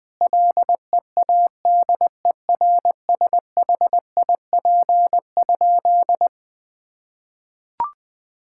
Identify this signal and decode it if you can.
Morse code, heard as a keyed tone, about 21 wpm, 700 Hz. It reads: LEADERSHIP?
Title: Sentences from Top 1600 Words 20wpm